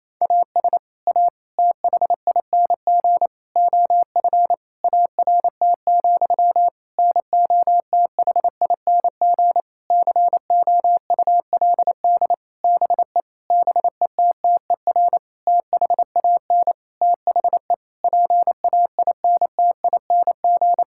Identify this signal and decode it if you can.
Morse code, heard as a keyed tone, about 28 wpm, 700 Hz. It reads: AS A THING OF ART, NOTHING COULD BE BETTER THAN THE PAINTING
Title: Sentences from Top 1000 Words - Review Part 6 - 28wpm